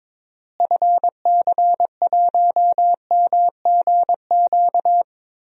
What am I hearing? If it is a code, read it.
Morse code, heard as a keyed tone, about 21 wpm, 700 Hz. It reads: FC1MGQ
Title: DXCC-100 Call Signs - 6 Characters - Repeated 2x - 22wpm